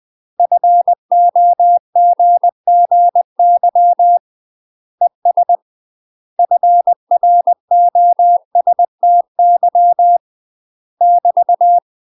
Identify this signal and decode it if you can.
Morse code, heard as a keyed tone, about 20 wpm, 700 Hz. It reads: FOGGY ES FROSTY <BT>
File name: UK MASTER QSOs TRAINING / 20WPM / UK MASTER QSOs 0053